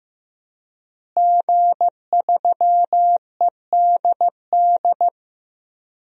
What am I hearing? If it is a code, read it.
Morse code, heard as a keyed tone, about 15 wpm, 700 Hz. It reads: G3EDD
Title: DXCC-100 Call Signs - 5 Characters - Repeated 3x - 15wpm